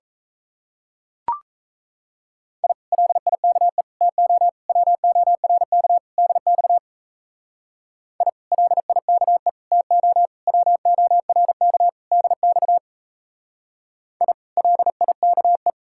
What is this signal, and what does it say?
Morse code, heard as a keyed tone, about 33 wpm, 700 Hz. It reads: I LIKE TO WORK DX I LIKE TO WORK DX I LIKE
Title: Speed-Racing Common QSO Phrases - 28wpm